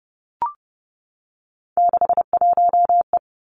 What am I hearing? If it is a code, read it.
Morse code, heard as a keyed tone, about 30 wpm, 700 Hz. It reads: 61E
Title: Three Challenging Letter-Number Combinations 30wpm